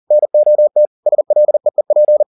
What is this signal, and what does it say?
Morse code, heard as a keyed tone, about 40 wpm, 600 Hz. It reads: NOT SLEEP
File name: the old man and the sea 2 40wpm